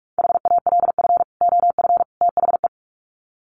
Morse code, heard as a keyed tone, about 44 wpm, 700 Hz. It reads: HALF OF THE
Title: Sets of 3 Words 45wpm